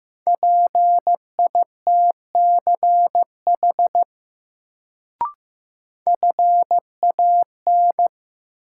Morse code, heard as a keyed tone, about 15 wpm, 700 Hz. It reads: PITCH FAN
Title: Sentences from Top 1300 Words - New Word Drill Alt - 15wpm